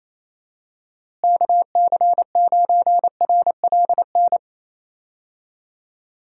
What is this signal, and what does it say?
Morse code, heard as a keyed tone, about 28 wpm, 700 Hz. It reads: KC9RLN
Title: USA Call Signs - 6 Characters - Repeated 2x - 28wpm